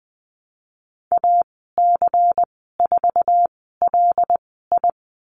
Morse code, heard as a keyed tone, about 20 wpm, spaced as for 15 wpm, 700 Hz. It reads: AC4LI
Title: 128 Lesson 21g - Number 6 - Call Signs 20wpm